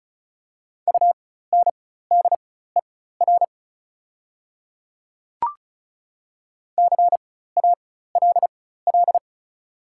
Morse code, heard as a keyed tone, about 34 wpm, spaced as for 16 wpm, 700 Hz. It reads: UNDER CALL
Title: 4x Character Spacing - Top 200 Words - 35wpm